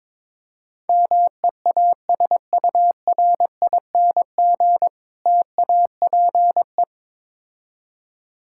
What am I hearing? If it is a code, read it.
Morse code, heard as a keyed tone, about 22 wpm, 700 Hz. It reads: MEASURING TAPE
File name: Difficult Words with Hints - 22wpm